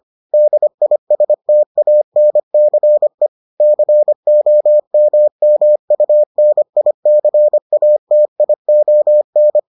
Morse code, heard as a keyed tone, about 25 wpm, 600 Hz. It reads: DISTANCE COMMUNICATION